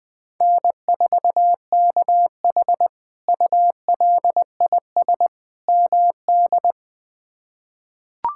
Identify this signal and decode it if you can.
Morse code, heard as a keyed tone, about 20 wpm, 700 Hz. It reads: N4KH ULIS MD
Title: K1USN SST QSO Practice - 20wpm